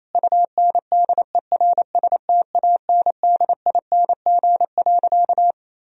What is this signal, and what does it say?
Morse code, heard as a keyed tone, about 28 wpm, 700 Hz. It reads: UNDERSTANDING.
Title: Sentences from Top 1000 Words - Review Part 3 - 28wpm